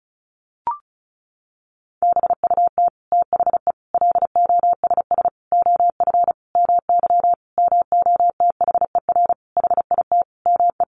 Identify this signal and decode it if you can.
Morse code, heard as a keyed tone, about 34 wpm, 700 Hz. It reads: BUT THE LOSS OF MY MOTHER HIT ME
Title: Sentences from Top 1400 Words 35wpm